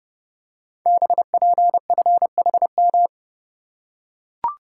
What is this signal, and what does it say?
Morse code, heard as a keyed tone, about 30 wpm, 700 Hz. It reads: BPFHM
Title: Code Groups - 5 Letters - 30wpm